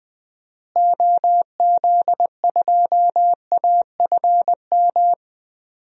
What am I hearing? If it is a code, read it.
Morse code, heard as a keyed tone, about 20 wpm, 700 Hz. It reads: OZ2AFM
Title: DXCC-100 Call Signs - 6 Characters - Repeated 2x - 20wpm